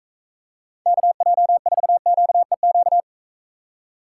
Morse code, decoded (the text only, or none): KJ4QEQ